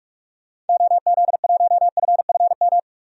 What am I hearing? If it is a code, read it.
Morse code, heard as a keyed tone, about 47 wpm, 700 Hz. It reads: OZ1FFM
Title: DXCC-100 Call Signs - 6 Characters - 45wpm